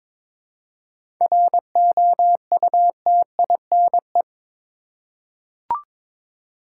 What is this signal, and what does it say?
Morse code, heard as a keyed tone, about 22 wpm, 700 Hz. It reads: ROUTINE
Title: Seven Letter Words - 22wpm